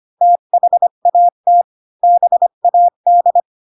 Morse code, heard as a keyed tone, about 25 wpm, 700 Hz. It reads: THAT BAD
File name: Pinocchio / Pinocchio ch4 20 25 ik1hge